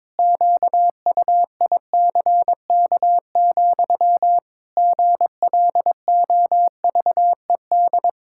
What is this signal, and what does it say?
Morse code, heard as a keyed tone, about 22 wpm, 700 Hz. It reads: QUICK, GLOVED
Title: Pangram Challenge v2 - 22wpm